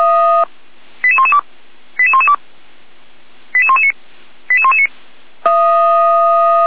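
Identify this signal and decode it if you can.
Five-tone selective call (ZVEI), heard as quick runs of five tones, tones of about 70 ms. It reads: 88112 88112 88119 88119
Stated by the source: ZVEI five-tone call received by SDR.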